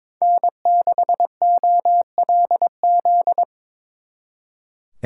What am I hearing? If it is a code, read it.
Morse code, heard as a keyed tone, about 25 wpm, 700 Hz. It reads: N6OLZ
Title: Speed Racing - USA 1x3 Call Signs 22wpm